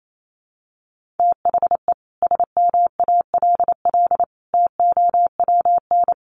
Morse code, heard as a keyed tone, about 28 wpm, 700 Hz. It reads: THE SMALL TOWN
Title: Sentences from Top 1300 Words 28wpm